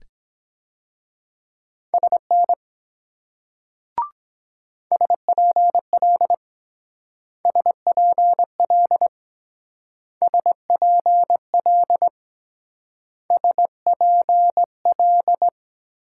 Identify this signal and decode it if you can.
Morse code, heard as a keyed tone, about 22 wpm, 700 Hz. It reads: SN SPL SPL SPL SPL